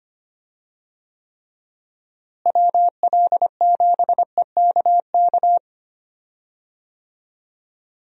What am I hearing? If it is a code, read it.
Morse code, heard as a keyed tone, about 25 wpm, 700 Hz. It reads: WL7EKK
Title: USA Call Signs - 6 Characters - Repeated 3x - 25wpm